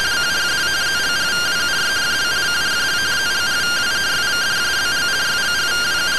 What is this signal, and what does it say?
Two-tone teleprinter signal, about 170 Hz shift.